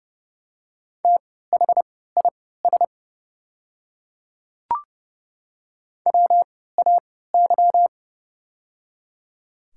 Morse code, heard as a keyed tone, about 30 wpm, spaced as for 17 wpm, 700 Hz. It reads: THIS WAY
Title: 3x Character Spacing - Top 100 Words - 30wpm